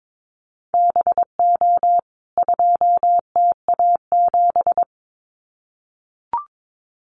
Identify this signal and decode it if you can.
Morse code, heard as a keyed tone, about 22 wpm, 700 Hz. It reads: BO 2TA7
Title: CWT Exchange 22wpm